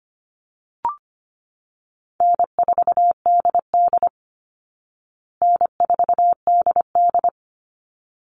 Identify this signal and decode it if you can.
Morse code, heard as a keyed tone, about 25 wpm, 700 Hz. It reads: N4DD N4DD
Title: Call Signs Repeated 3x - 25wpm